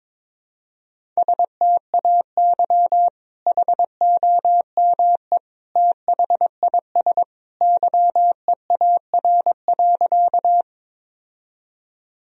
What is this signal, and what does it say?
Morse code, heard as a keyed tone, about 22 wpm, 700 Hz. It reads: STAY HOME THIS YEAR.
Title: Sentences from Top 1000 Words - Review Part 1 - 22wpm